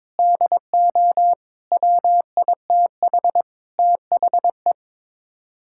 Morse code, heard as a keyed tone, about 21 wpm, 700 Hz. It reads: DO WITH THE
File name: practice / 22wpm / Sets of 7 Words - 22wpm